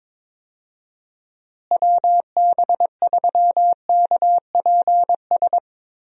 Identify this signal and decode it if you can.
Morse code, heard as a keyed tone, about 22 wpm, 700 Hz. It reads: WB3KPS
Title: USA Call Signs - 6 Characters - Repeated 2x - 22wpm